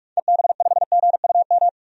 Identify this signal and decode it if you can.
Morse code, heard as a keyed tone, about 41 wpm, 700 Hz. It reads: EB5GUM
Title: DXCC-100 Call Signs - 6 Characters - 45wpm